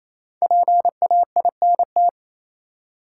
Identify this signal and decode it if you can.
Morse code, heard as a keyed tone, about 28 wpm, 700 Hz. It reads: PAINT
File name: Easy Words - Speed-Racing with Hints - 22wpm